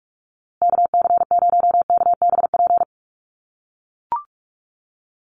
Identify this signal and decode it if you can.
Morse code, heard as a keyed tone, about 44 wpm, 700 Hz. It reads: KC0KBP
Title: USA Call Signs Repeated 3x 45wpm